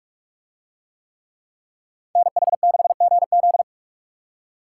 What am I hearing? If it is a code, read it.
Morse code, heard as a keyed tone, about 42 wpm, 700 Hz. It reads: NH6GZ